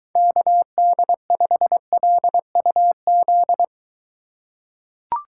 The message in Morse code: KD5LUZ

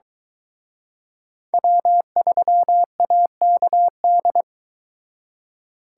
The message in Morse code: W3AKD

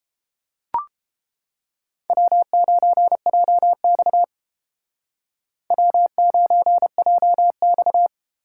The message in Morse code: W9JX W9JX

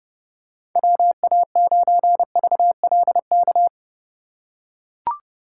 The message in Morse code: WA9VLK

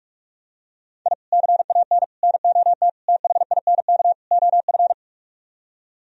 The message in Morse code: I CAN NOT THINK OF

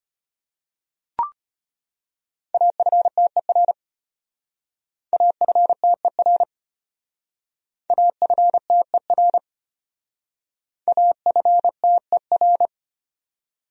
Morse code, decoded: AFTER AFTER AFTER AFTER

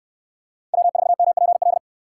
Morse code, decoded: W4MPL

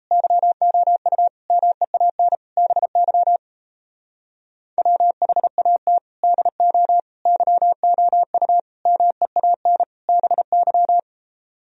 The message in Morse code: YOU MEAN BY WHAT DO YOU MEAN BY

